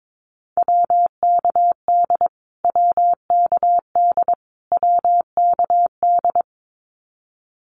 Morse code, heard as a keyed tone, about 22 wpm, 700 Hz. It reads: WKD WKD WKD
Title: Abbreviations Repeated 3x - 22wpm